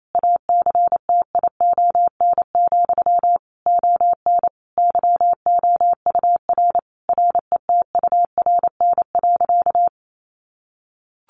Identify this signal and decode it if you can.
Morse code, heard as a keyed tone, about 28 wpm, 700 Hz. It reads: ACTION, ON YOUR RETURN.
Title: A Sound Of Thunder 28wpm